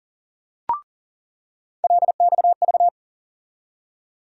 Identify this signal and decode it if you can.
Morse code, heard as a keyed tone, about 40 wpm, 700 Hz. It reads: LXV
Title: USA Call Sign Postfixes - 3 Letters 40wpm